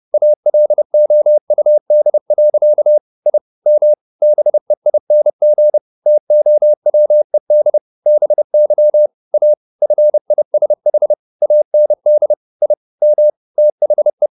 Morse code, heard as a keyed tone, about 30 wpm, 600 Hz. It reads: ALOUD. I M BEING TOWED BY A FISH AND I M THE
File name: the old man and the sea / the old man and the sea 1 30wpm